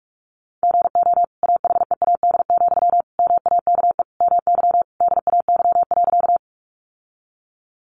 Morse code, heard as a keyed tone, about 44 wpm, 700 Hz. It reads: GO AHEAD, MAKE MY DAY.
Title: Sentences from Top 1100 Words 45wpm